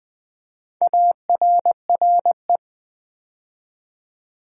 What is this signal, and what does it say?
Morse code, heard as a keyed tone, about 20 wpm, 700 Hz. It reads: ARRE
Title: Build A Word - 2 Letters At A Time - 6-Letter Words - 20wpm